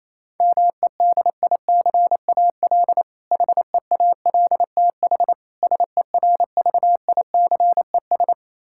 Morse code, heard as a keyed tone, about 27 wpm, 700 Hz. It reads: MEDICAL HEALTH SERVICES
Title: K1USN SST QSO Practice 28wpm